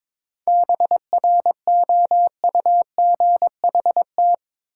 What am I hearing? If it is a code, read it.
Morse code, heard as a keyed tone, about 22 wpm, 700 Hz. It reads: BROUGHT
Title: Sentences from Top 500 Words - New Word Drill - 22wpm